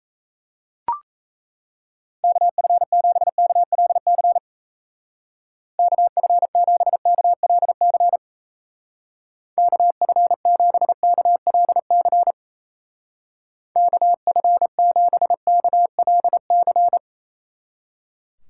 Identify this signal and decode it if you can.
Morse code, heard as a keyed tone, about 36 wpm, 700 Hz. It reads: KF7KLC KF7KLC KF7KLC KF7KLC